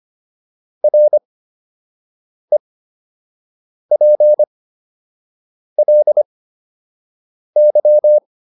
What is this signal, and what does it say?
Morse code, heard as a keyed tone, about 25 wpm, spaced as for 6 wpm, 600 Hz. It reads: REPLY